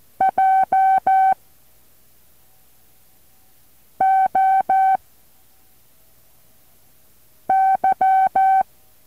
Morse code, heard as a keyed tone, about 14 wpm, 770 Hz. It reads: J O Y
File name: intl morse code2